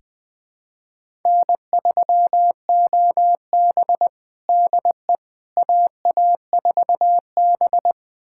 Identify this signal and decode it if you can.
Morse code, heard as a keyed tone, about 20 wpm, 700 Hz. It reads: N3OB DE AA4B
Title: USA Callsign DE Callsign - 4 Characters - 20wpm